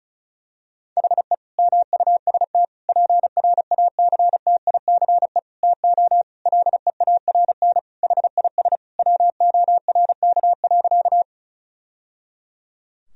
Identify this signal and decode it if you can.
Morse code, heard as a keyed tone, about 35 wpm, 700 Hz. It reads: HE MUST PRACTICE TO LEARN HIS WORK.
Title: Sentences from Top 1000 Words - Review Part 4 - 35wpm